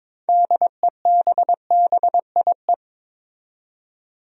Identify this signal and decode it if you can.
Morse code, heard as a keyed tone, about 22 wpm, 700 Hz. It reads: DEBBIE
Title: USA Common Names - 22wpm